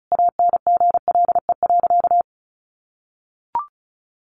Morse code, heard as a keyed tone, about 34 wpm, 700 Hz. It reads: ANGLE.
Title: Sentences from Top 1500 Words 35wpm